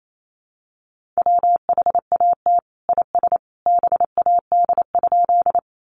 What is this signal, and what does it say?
Morse code, heard as a keyed tone, about 28 wpm, 700 Hz. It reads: WHAT IS BAD?
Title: Sentences from Top 700 Words 28wpm